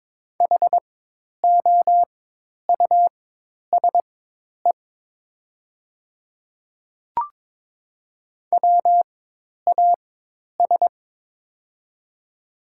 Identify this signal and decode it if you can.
Morse code, heard as a keyed tone, about 22 wpm, spaced as for 10 wpm, 700 Hz. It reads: HOUSE WAS
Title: 4x Character Spacing - Top 200 Words - 22wpm